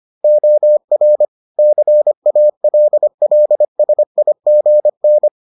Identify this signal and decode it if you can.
Morse code, heard as a keyed tone, about 25 wpm, 600 Hz. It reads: OR CALLSIGN